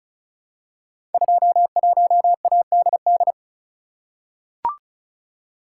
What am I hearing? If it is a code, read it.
Morse code, heard as a keyed tone, about 36 wpm, 700 Hz. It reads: 21ADD
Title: Code Groups - 5 Letter:Numbers - 35wpm